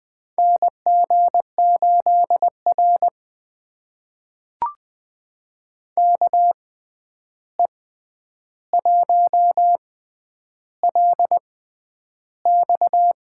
Morse code, heard as a keyed tone, about 20 wpm, spaced as for 10 wpm, 700 Hz. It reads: NG8R KE1LX
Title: USA Call Signs - 6x Character Spacing - 20wpm